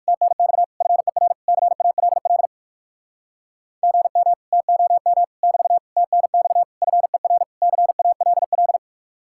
Morse code, heard as a keyed tone, about 64 wpm, spaced as for 56 wpm, 700 Hz. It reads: TNX FER CALL GM TOM <BT> TNX FER CALL
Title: Speed-Racing Common QSO Phrases - 45wpm